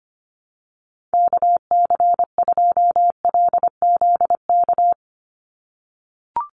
Morse code, heard as a keyed tone, about 25 wpm, 700 Hz. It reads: KC2LZK
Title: USA Call Signs 25wpm